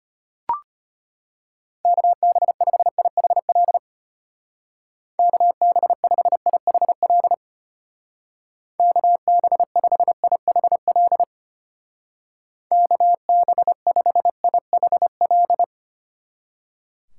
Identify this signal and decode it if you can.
Morse code, heard as a keyed tone, about 32 wpm, 700 Hz. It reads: KB5IHL KB5IHL KB5IHL KB5IHL